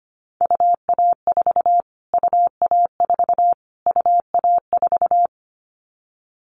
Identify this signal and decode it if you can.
Morse code, heard as a keyed tone, about 25 wpm, 700 Hz. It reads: UA4 UA4 UA4
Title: Callsign Prefixes for Top 25 DXCC - Repeated 3x - 25wpm